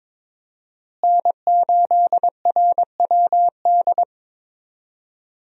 Morse code, heard as a keyed tone, about 22 wpm, 700 Hz. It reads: N8RWD
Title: USA Call Signs - 5 Characters - Repeated 2x - 22wpm